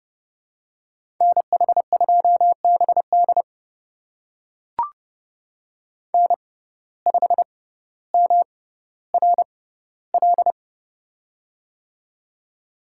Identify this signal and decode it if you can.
Morse code, heard as a keyed tone, about 31 wpm, spaced as for 16 wpm, 700 Hz. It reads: NH2BD N5MRL
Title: USA Call Signs - 5 Characters - 6x Character Spacing - 30wpm